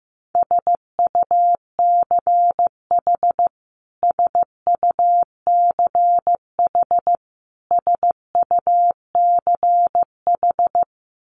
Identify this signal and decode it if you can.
Morse code, heard as a keyed tone, about 15 wpm, 700 Hz. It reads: SUCH SUCH SUCH
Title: Top 500 Words Repeated 15wpm